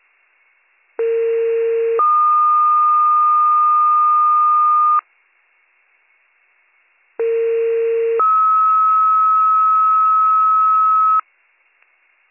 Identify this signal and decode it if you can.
Two-tone page, heard as a one-second tone, then a three-second tone, about 460 then 1220 Hz, about 460 then 1290 Hz. It is heard from 1.0 to 11.2 s.